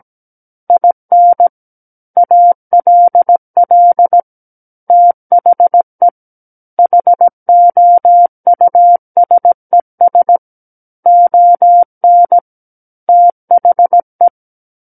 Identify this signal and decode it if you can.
Morse code, heard as a keyed tone, about 17 wpm, 700 Hz. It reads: IN ALL THE HOUSES ON THE
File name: Book One / 08-Chapter 8 - 17 WPM